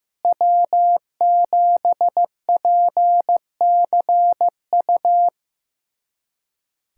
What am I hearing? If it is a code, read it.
Morse code, heard as a keyed tone, about 15 wpm, 700 Hz. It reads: W7PCU
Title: USA Callsign DE Callsign - 5 Characters - 15wpm